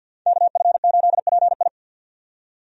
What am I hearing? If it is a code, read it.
Morse code, heard as a keyed tone, about 47 wpm, 700 Hz. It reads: KF8PI
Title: USA Call Signs - 5 Characters - 50wpm